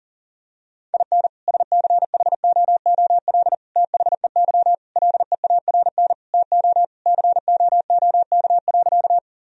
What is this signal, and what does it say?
Morse code, heard as a keyed tone, about 37 wpm, 700 Hz. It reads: IN SCHOOL THEY LEARN TO COOK.
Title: Sentences from Top 1000 Words - Review Part 4 - 40wpm